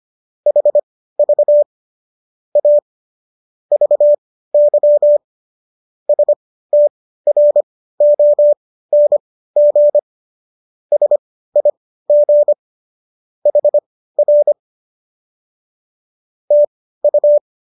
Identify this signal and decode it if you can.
Morse code, heard as a keyed tone, about 25 wpm, spaced as for 15 wpm, 590 Hz. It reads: HV A VY STRONG SIG HR TU